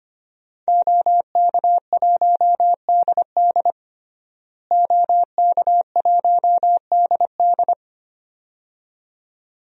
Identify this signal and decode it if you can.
Morse code, heard as a keyed tone, about 25 wpm, 700 Hz. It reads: OK1DD OK1DD
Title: DXCC-100 Call Signs - 5 Characters - Repeated 2x - 25wpm